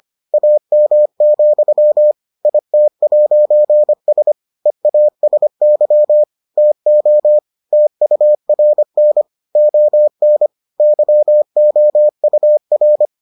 Morse code, heard as a keyed tone, about 25 wpm, 600 Hz. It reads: AM, IT'S EASY TO TURN ON YOUR